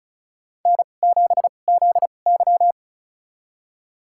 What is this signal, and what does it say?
Morse code, heard as a keyed tone, about 36 wpm, spaced as for 25 wpm, 700 Hz. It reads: N7ZY
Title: USA Call Signs - 2x Character Spacing - 35wpm